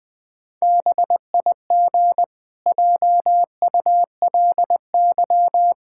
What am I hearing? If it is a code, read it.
Morse code, heard as a keyed tone, about 20 wpm, 700 Hz. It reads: BIG JULY
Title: Pangram Challenge v2 - 20wpm